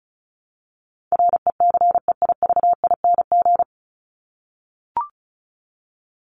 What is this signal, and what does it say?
Morse code, heard as a keyed tone, about 34 wpm, 700 Hz. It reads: RECEIVING
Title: ING Suffix - Top 100 Words with Base Words 35wpm